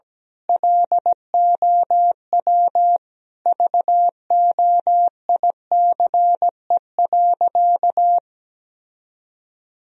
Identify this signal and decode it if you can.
Morse code, heard as a keyed tone, about 17 wpm, 700 Hz. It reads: LOW VOICE.
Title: Sentences from Top 1000 Words - Review Part 1 - 17wpm